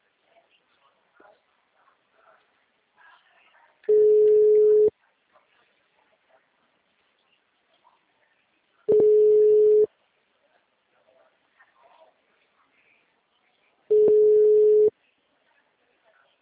Ringback tone (Europe, 425 Hz).
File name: out- -2012- - - .46718